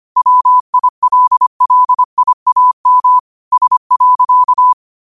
Morse code, heard as a keyed tone, about 25 wpm, 1000 Hz. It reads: WILLIAM S.